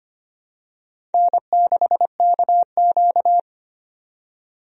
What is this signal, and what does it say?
Morse code, heard as a keyed tone, about 25 wpm, 700 Hz. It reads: N6KQ
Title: USA Call Signs - 4 Characters - 25wpm